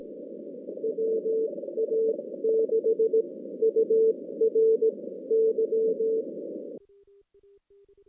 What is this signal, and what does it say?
Morse code, heard as a keyed tone, about 18 wpm, spaced as for 14 wpm, 450 Hz. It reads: WA6URY
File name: 2016 JAN CW NAQP / 2016 Jan naqp k5ot - 0-11